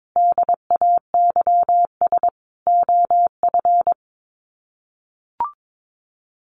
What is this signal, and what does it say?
Morse code, heard as a keyed tone, about 22 wpm, 700 Hz. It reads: DAYS OF
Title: Sets of 5 Words Encore 22wpm